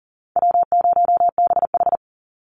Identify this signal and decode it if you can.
Morse code, heard as a keyed tone, about 40 wpm, 700 Hz. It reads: W0BH
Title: Call Signs - 40wpm